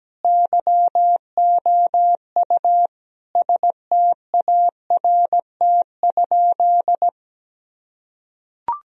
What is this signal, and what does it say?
Morse code, heard as a keyed tone, about 17 wpm, 700 Hz. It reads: YOU START?
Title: Sentences from Top 1000 Words - Review Part 1 - 17wpm